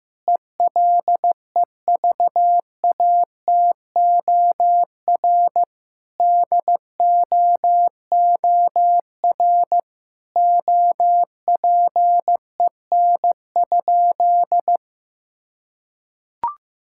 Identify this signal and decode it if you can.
Morse code, heard as a keyed tone, about 15 wpm, 700 Hz. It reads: ELEVATOR DOOR OPEN?
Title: Sentences from Top 1600 Words 15wpm